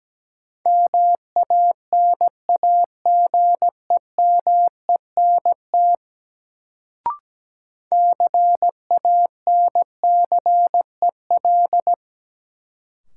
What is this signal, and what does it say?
Morse code, heard as a keyed tone, about 17 wpm, 700 Hz. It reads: MANAGEMENT CANCEL
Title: Sentences from Top 1200 Words - New Word Drill Alt - 17wpm